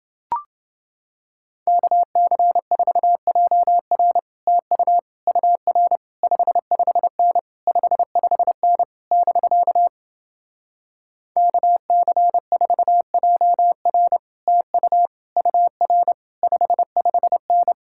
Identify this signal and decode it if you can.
Morse code, heard as a keyed tone, about 29 wpm, 700 Hz. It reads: KC4JR TU UR 55N 55N <BK> KC4JR TU UR 55N